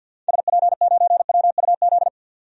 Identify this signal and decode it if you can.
Morse code, heard as a keyed tone, about 49 wpm, 700 Hz. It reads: SP9WUZ
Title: DXCC-100 Call Signs - 6 Characters - 50wpm